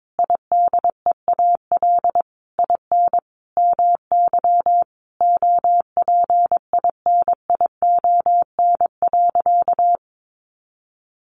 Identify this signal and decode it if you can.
Morse code, heard as a keyed tone, about 22 wpm, 700 Hz. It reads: IDEAL IN MY OPINION.
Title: Sentences from Top 1400 Words 22wpm